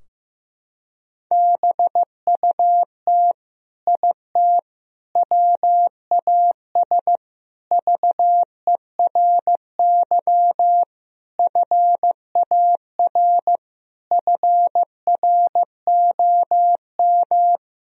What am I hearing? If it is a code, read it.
Morse code, heard as a keyed tone, about 15 wpm, 700 Hz. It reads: BUT IT WAS VERY FAR FROM
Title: Sentences from Top 1000 Words - Review Part 2 - 15wpm